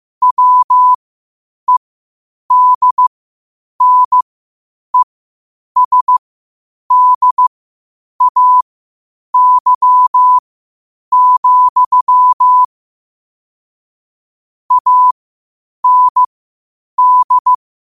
Morse code, heard as a keyed tone, about 15 wpm, spaced as for 8 wpm, 1000 Hz. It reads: WEDNESDAY, AND
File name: qotdfiles / 8wpm